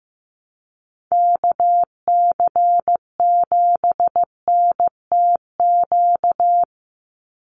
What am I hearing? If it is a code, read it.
Morse code, heard as a keyed tone, about 15 wpm, 700 Hz. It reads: KC7NTQ
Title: USA Call Signs 15wpm